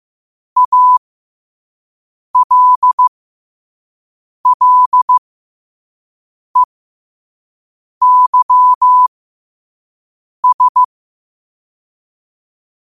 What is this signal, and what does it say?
Morse code, heard as a keyed tone, about 15 wpm, spaced as for 5 wpm, 1000 Hz. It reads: ALLEYS
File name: qotdfiles / 5wpm